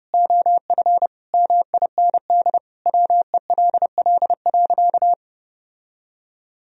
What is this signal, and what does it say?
Morse code, heard as a keyed tone, about 30 wpm, 700 Hz. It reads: OF MIND WELL.
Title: Sentences from Top 1000 Words - Review Part 2 - 30wpm